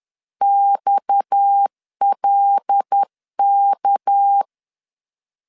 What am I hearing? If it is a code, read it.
Morse code, heard as a keyed tone, about 10 wpm, 800 Hz. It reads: XLK
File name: LEZIONI / 10 SIGNS / 10C DOT 10wpm 2940